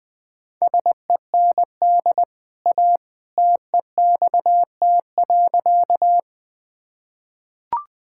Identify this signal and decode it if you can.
Morse code, heard as a keyed tone, about 20 wpm, 700 Hz. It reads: SEND A TEXT.
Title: Easy Sentences with Hints - 20wpm